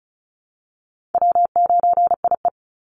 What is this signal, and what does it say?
Morse code, heard as a keyed tone, about 34 wpm, 700 Hz. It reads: W9IE
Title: Call Signs Repeated 2x - 35wpm